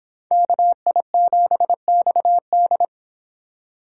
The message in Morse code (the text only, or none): KI7XD